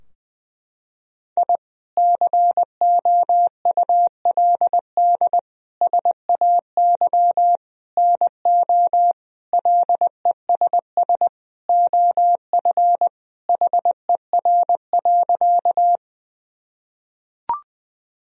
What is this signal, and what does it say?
Morse code, heard as a keyed tone, about 21 wpm, 700 Hz. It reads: I COULD SAY NO LESS OF HER.
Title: Sentences from Top 1000 Words - Review Part 2 - 20wpm